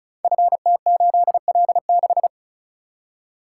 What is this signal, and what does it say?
Morse code, heard as a keyed tone, about 36 wpm, 700 Hz. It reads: FT8L6
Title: Code Groups - 5 Letter:Numbers - 35wpm